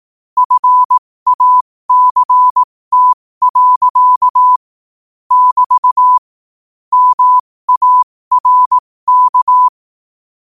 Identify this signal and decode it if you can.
Morse code, heard as a keyed tone, about 18 wpm, spaced as for 15 wpm, 1000 Hz. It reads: FACT. <BT> MARK
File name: qotdfiles / 15wpm